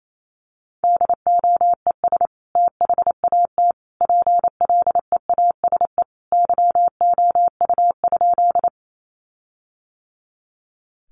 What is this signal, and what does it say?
Morse code, heard as a keyed tone, about 28 wpm, 700 Hz. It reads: DOES THAT PLEASE YOU?
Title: Sentences from Top 1500 Words 28wpm